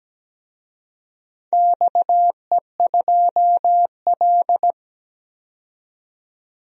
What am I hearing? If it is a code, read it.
Morse code, heard as a keyed tone, about 17 wpm, 700 Hz. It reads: XE2L
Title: DXCC-100 Call Signs - 4 Characters - 17wpm